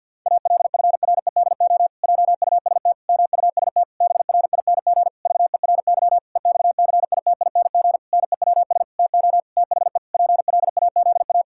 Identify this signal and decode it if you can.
Morse code, heard as a keyed tone, about 52 wpm, 700 Hz. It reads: ALFREDO JUST MUST BRING VERY EXCITING NEWS TO THE PLAZA
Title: Pangram Challenge v2 - 50wpm